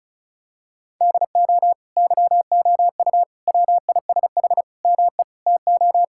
Morse code, heard as a keyed tone, about 34 wpm, 700 Hz. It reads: DO YOU WISH ME TO
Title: Sentences from Top 1600 Words 35wpm